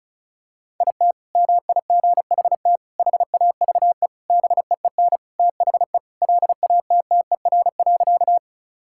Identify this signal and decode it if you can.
Morse code, heard as a keyed tone, about 36 wpm, 700 Hz. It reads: IT MIGHT HAVE BEEN THE LATTER.
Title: Sentences from Top 1600 Words 35wpm